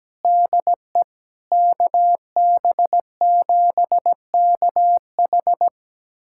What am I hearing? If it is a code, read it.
Morse code, heard as a keyed tone, about 17 wpm, 700 Hz. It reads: DE KB7KH
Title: USA Callsign DE Callsign 17wpm